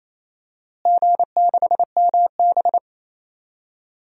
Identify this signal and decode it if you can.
Morse code, heard as a keyed tone, about 28 wpm, 700 Hz. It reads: G6MB
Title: DXCC-100 Call Signs - 4 Characters - Repeated 2x - 28wpm